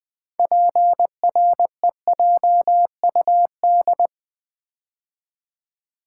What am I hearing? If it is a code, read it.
Morse code, heard as a keyed tone, about 20 wpm, 700 Hz. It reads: PREJUD
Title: Build A Word - 3 Letters At A Time - 9-Letter Words - 20wpm